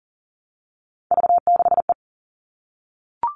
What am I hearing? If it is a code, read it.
Morse code, heard as a keyed tone, about 40 wpm, 700 Hz. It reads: V6E
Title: All Cut Numbers - 3-Digit 40wpm